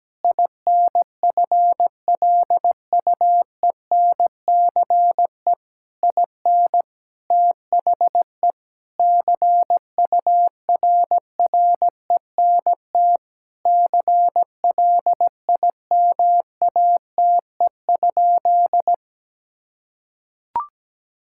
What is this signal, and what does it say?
Morse code, heard as a keyed tone, about 17 wpm, 700 Hz. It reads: INFLUENCE IN THE CURRENT CLIMATE?
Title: Sentences from Top 1600 Words 17wpm